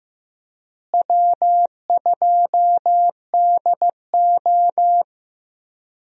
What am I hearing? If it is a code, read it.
Morse code, heard as a keyed tone, about 15 wpm, 700 Hz. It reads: W2DO
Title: USA Call Signs - 4 Characters - Repeated 3x - 15wpm